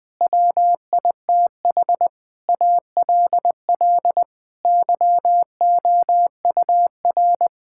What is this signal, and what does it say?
Morse code, heard as a keyed tone, about 20 wpm, 700 Hz. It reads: WITH ALL YOUR
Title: Sets of 9 Words - 20wpm